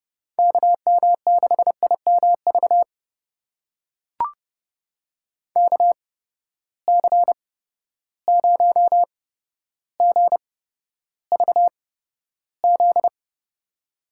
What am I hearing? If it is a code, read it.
Morse code, heard as a keyed tone, about 32 wpm, spaced as for 13 wpm, 700 Hz. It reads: KM6IMV KC0GVZ
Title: USA Call Signs - 6 Characters - 8x Character Spacing - 30wpm